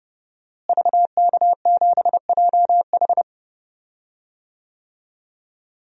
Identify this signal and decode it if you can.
Morse code, heard as a keyed tone, about 30 wpm, 700 Hz. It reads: VK7JH
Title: DXCC-100 Call Signs - 5 Characters - Repeated 2x - 30wpm